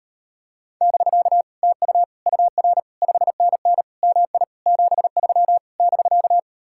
Morse code, heard as a keyed tone, about 39 wpm, 700 Hz. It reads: <BK> TU UR 5NN MI 73 <BK>